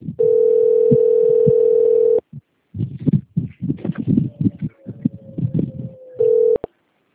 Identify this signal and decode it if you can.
Ringback tone (North America), 440 + 480 Hz.